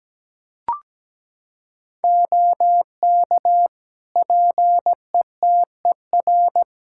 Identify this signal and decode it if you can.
Morse code, heard as a keyed tone, about 17 wpm, 700 Hz. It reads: OK PETER
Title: K1USN SST QSO Practice 17wpm